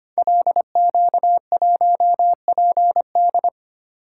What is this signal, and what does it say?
Morse code, heard as a keyed tone, about 25 wpm, 700 Hz. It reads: LQ1PD
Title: Code Groups - 5 Letter:Numbers - 25wpm